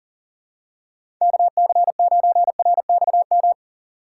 Morse code, heard as a keyed tone, about 40 wpm, 700 Hz. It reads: KC9RXM
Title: USA Call Signs - 6 Characters - 3x Character Spacing - 40wpm